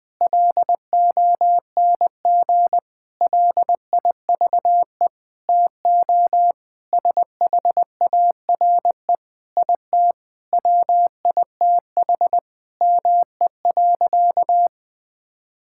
Morse code, heard as a keyed tone, about 20 wpm, 700 Hz. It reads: LONG LIVE TO SHARE IT WITH ME.
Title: Sentences from Top 1000 Words - Review Part 6 - 20wpm